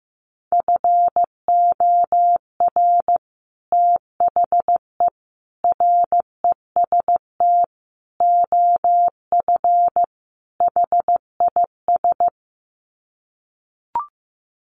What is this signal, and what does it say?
Morse code, heard as a keyed tone, about 15 wpm, 700 Hz. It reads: FOR THE REST OF HIS
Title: Sets of 5 Words 15wpm